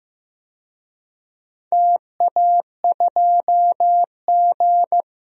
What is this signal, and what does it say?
Morse code, heard as a keyed tone, about 15 wpm, 700 Hz. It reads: TA2G
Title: DXCC-100 Call Signs - 4 Characters - 15wpm